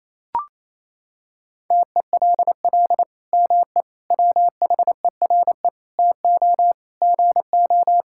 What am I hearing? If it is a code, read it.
Morse code, heard as a keyed tone, about 27 wpm, 700 Hz. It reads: TELL ME WHERE TO GO
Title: Sentences from Top 1600 Words 28wpm